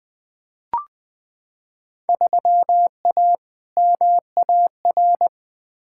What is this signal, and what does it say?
Morse code, heard as a keyed tone, about 20 wpm, 700 Hz. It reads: 3A MAR
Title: Field Day Exchanges Encore - 20wpm